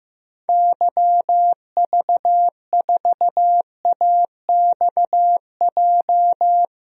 Morse code, heard as a keyed tone, about 15 wpm, 700 Hz. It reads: YV4AXJ
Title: DXCC-100 Call Signs - 6 Characters - 15wpm